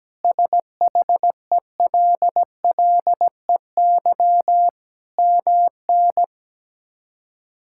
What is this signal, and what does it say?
Morse code, heard as a keyed tone, about 17 wpm, 700 Hz. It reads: SHELLEY MN
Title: K1USN SST QSO Practice - 17wpm